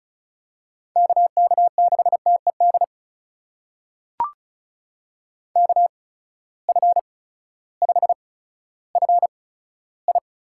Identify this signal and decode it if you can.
Morse code, heard as a keyed tone, about 36 wpm, spaced as for 16 wpm, 700 Hz. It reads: KK6TED KF5FI
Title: USA Call Signs - 6 Characters - 8x Character Spacing - 35wpm